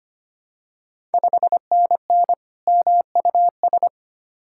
Morse code, heard as a keyed tone, about 25 wpm, 690 Hz. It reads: 5NN MUS
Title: Ontario QSO Party Exchange 25wpm